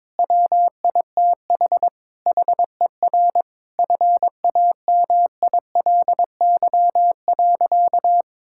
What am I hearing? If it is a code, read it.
Morse code, heard as a keyed tone, about 22 wpm, 700 Hz. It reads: WITH HER FAMILY.
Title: Sentences from Top 1000 Words - Review Part 6 - 22wpm